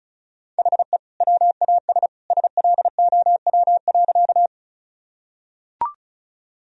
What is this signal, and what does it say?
Morse code, heard as a keyed tone, about 34 wpm, 700 Hz. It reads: HE WAS SLOW.
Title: Sentences from Top 1000 Words - Review Part 2 - 35wpm